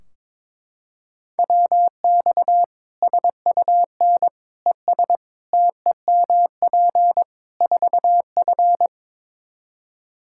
Morse code, heard as a keyed tone, about 22 wpm, 700 Hz. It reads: WX SUN ES TEMP 4F
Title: Practice QSO - Exchange 2 - Info - 22wpm